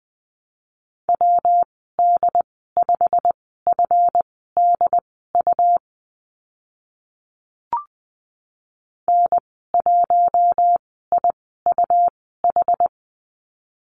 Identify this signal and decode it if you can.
Morse code, heard as a keyed tone, about 20 wpm, spaced as for 15 wpm, 700 Hz. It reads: WD5FDU N1IUH
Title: 147 Lesson 24g - Letter y - Call Signs 20wpm